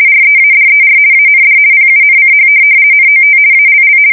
RTTY, about 50 baud. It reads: READER AND PUNCH NOT ALL
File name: fsk170-50B / teletype-corp